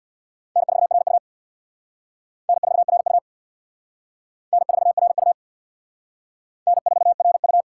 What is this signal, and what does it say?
Morse code, heard as a keyed tone, about 60 wpm, 700 Hz. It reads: N4RU N4RU N4RU N4RU
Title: Speed Racing - USA 1x2 Call Signs 50wpm